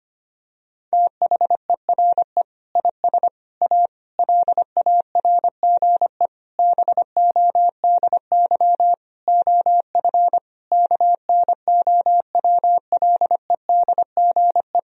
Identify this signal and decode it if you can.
Morse code, heard as a keyed tone, about 25 wpm, 700 Hz. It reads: THERE IS A LARGE BODY OF KNOWLEDGE
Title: Sentences from Top 1000 Words - Review Part 4 - 25wpm